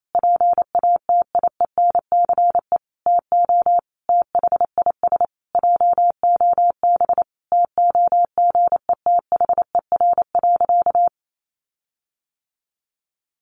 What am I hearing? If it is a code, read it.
Morse code, heard as a keyed tone, about 28 wpm, 700 Hz. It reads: PATIENCE TO THIS JOB TOGETHER.
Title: Sentences from Top 1300 Words 28wpm